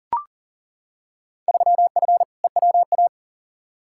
Morse code, heard as a keyed tone, about 40 wpm, 700 Hz. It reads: 3F EWA
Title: Field Day Exchanges Encore - 40wpm